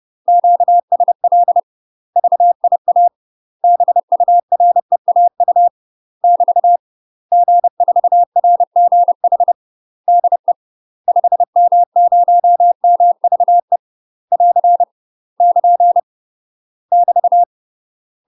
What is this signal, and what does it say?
Morse code, heard as a keyed tone, about 30 wpm, 700 Hz. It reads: QSL VIA BUREAU <BT> G4RGH DE 5M0MVE <AR> <KN> <BT>
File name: UK MASTER QSOs TRAINING / 30WPM / UK MASTER QSOs 0022